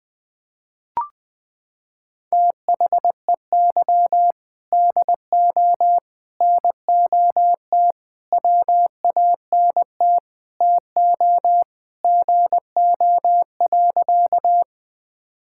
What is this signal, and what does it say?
Morse code, heard as a keyed tone, about 21 wpm, 700 Hz. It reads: THEY DO NOT WANT TO GO.
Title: Sentences with 4-Word Phrases - 20wpm